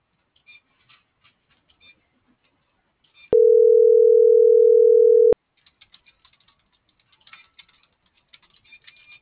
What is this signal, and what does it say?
Ringback tone (North America), 440 + 480 Hz.